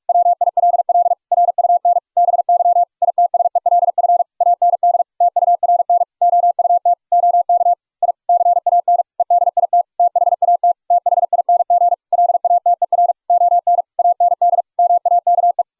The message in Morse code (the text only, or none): WILL RUN BY ITSELF AND TURN OUT OK I CAN EDIT THAT THING LATER ON AND MAKE